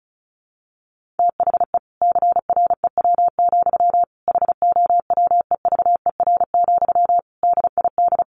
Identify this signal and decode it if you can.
Morse code, heard as a keyed tone, about 34 wpm, 700 Hz. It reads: THE CREW, HOWEVER, DID
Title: Sentences from Top 1500 Words 35wpm